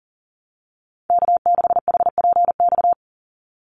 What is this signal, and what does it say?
Morse code, heard as a keyed tone, about 40 wpm, 700 Hz. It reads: K6HPX
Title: NA SOTA Activator Callsigns 40wpm